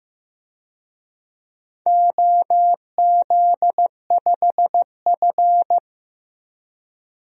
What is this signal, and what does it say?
Morse code, heard as a keyed tone, about 15 wpm, 700 Hz. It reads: OZ5F
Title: DXCC-100 Call Signs - 4 Characters - Repeated 3x - 15wpm